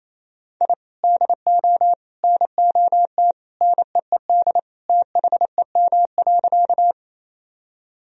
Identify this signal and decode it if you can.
Morse code, heard as a keyed tone, about 27 wpm, 700 Hz. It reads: I DO NOT NEED THEM.
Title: Sentences from Top 1000 Words - Review Part 1 - 28wpm